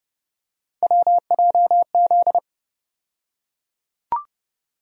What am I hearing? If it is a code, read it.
Morse code, heard as a keyed tone, about 30 wpm, 700 Hz. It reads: WJZ
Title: USA Call Sign Postfixes - 3 Letters 30wpm